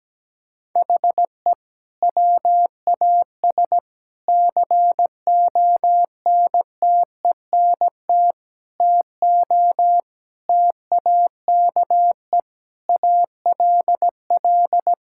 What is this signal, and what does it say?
Morse code, heard as a keyed tone, about 17 wpm, 700 Hz. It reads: HE WAS CONTENT TO TAKE ALL
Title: Sentences from Top 1000 Words - Review Part 6 - 17wpm